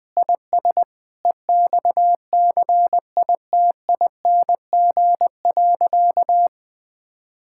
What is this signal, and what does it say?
Morse code, heard as a keyed tone, about 20 wpm, 700 Hz. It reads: IS EXCITING.
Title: Sentences with 6-Word Phrases - 20wpm